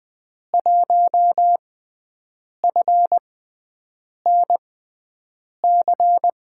Morse code, heard as a keyed tone, about 20 wpm, spaced as for 7 wpm, 700 Hz. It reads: 1FNC
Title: USA Call Signs - 5 Characters - 6x Character Spacing - 20wpm